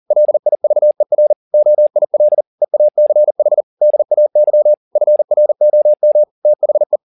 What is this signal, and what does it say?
Morse code, heard as a keyed tone, about 40 wpm, 600 Hz. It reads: LIVER OIL EACH DAY FROM THE